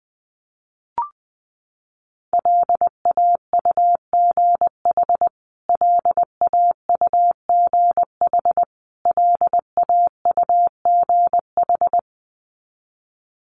Morse code, heard as a keyed tone, about 20 wpm, 700 Hz. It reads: LAUGH LAUGH LAUGH
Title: Top 500 Words Repeated 20wpm